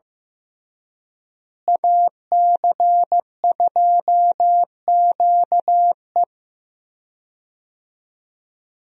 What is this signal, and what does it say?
Morse code, heard as a keyed tone, about 15 wpm, 700 Hz. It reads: AC2QE
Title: USA Call Signs - 5 Characters - Repeated 2x - 15wpm